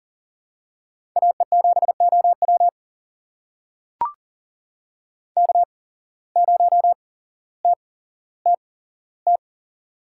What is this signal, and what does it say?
Morse code, heard as a keyed tone, about 39 wpm, spaced as for 17 wpm, 700 Hz. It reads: AE7OW K0TTT
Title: USA Call Signs - 5 Characters - 8x Character Spacing - 40wpm